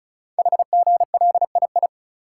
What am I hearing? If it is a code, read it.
Morse code, heard as a keyed tone, about 36 wpm, 690 Hz. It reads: HGLII
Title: Code Groups - 5 Letters - 35wpm